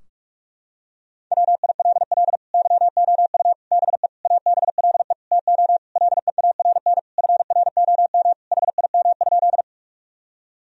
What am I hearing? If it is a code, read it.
Morse code, heard as a keyed tone, about 46 wpm, 700 Hz. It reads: WILL YOU BE ABLE TO LEARN FROM HIM?
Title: Sentences from Top 1000 Words - Review Part 2 - 45wpm